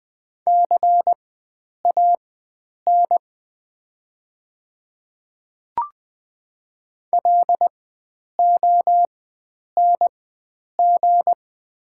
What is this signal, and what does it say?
Morse code, heard as a keyed tone, about 20 wpm, spaced as for 9 wpm, 690 Hz. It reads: CAN LONG
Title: 4x Character Spacing - Top 100 Words - 20wpm